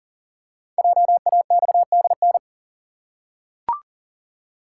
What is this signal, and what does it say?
Morse code, heard as a keyed tone, about 38 wpm, 690 Hz. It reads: JAXDN
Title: Code Groups - 5 Letters - 40wpm